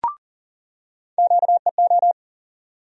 Morse code, heard as a keyed tone, about 40 wpm, 700 Hz. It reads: QEO
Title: USA Call Sign Postfixes - 3 Letters 40wpm